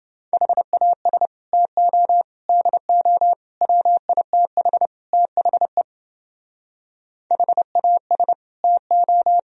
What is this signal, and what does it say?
Morse code, heard as a keyed tone, about 29 wpm, 700 Hz. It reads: HAS TO DO WITH THE HAS TO